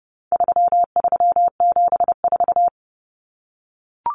Morse code, heard as a keyed tone, about 30 wpm, 700 Hz. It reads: 3374
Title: Numbers - 4-Digit 30wpm